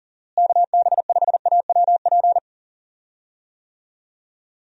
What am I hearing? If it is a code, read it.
Morse code, heard as a keyed tone, about 40 wpm, 700 Hz. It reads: KB5AWP
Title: USA Call Signs - 6 Characters - Repeated 2x - 40wpm